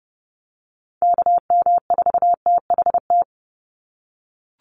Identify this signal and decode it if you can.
Morse code, heard as a keyed tone, about 29 wpm, 700 Hz. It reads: KM4THT
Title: USA Call Signs 30wpm